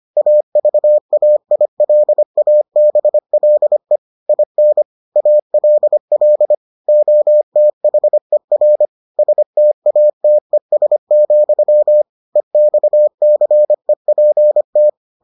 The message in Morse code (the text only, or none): AVAILABLE IN ALL OTHER STATES, EXCEPT